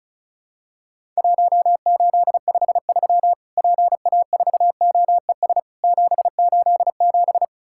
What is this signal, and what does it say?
Morse code, heard as a keyed tone, about 36 wpm, 700 Hz. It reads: 1853 PA4OES 787
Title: Worked All Europe DX Contest - QTC Exchange - 35wpm